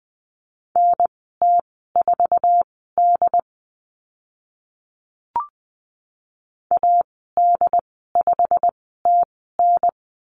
Morse code, heard as a keyed tone, about 20 wpm, spaced as for 15 wpm, 700 Hz. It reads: NT4D AD5TN
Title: 088 Lesson 15g - Number 5 - Call Signs 20wpm